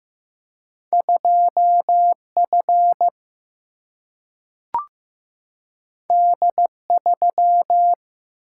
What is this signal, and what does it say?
Morse code, heard as a keyed tone, about 15 wpm, 700 Hz. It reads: 2 F D 3
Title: Single-Letter Single-Number 15wpm